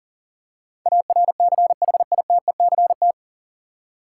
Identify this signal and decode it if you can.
Morse code, heard as a keyed tone, about 40 wpm, 700 Hz. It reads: ARCHITECT
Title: Practice QSO - Exchange 3 - More Info - 40wpm